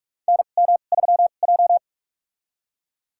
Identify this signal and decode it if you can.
Morse code, heard as a keyed tone, about 46 wpm, spaced as for 33 wpm, 700 Hz. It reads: NM3J
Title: USA Call Signs - 2x Character Spacing - 45wpm